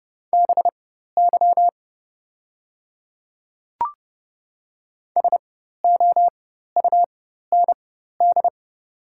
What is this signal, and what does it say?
Morse code, heard as a keyed tone, about 30 wpm, spaced as for 14 wpm, 690 Hz. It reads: BY SOUND
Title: 4x Character Spacing - Top 100 Words - 30wpm